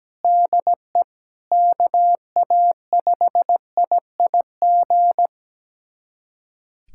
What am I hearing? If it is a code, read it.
Morse code, heard as a keyed tone, about 17 wpm, 700 Hz. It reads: DE KA5IIG
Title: USA Callsign DE Callsign - 6 Characters - 17wpm